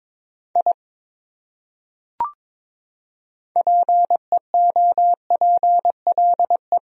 Morse code, heard as a keyed tone, about 22 wpm, 700 Hz. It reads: I PEOPLE
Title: Top 100 Words 22wpm